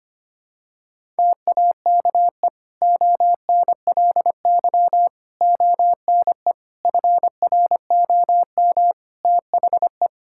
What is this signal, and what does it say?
Morse code, heard as a keyed tone, about 25 wpm, 700 Hz. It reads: TAKE ONLY ONE FROM THE
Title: Sentences from Top 1000 Words - Review Part 5 - 25wpm